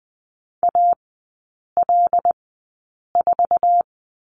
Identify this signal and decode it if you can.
Morse code, heard as a keyed tone, about 20 wpm, spaced as for 8 wpm, 700 Hz. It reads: AL4
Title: 087 Lesson 15e - Number 5 - Three Characters 20wpm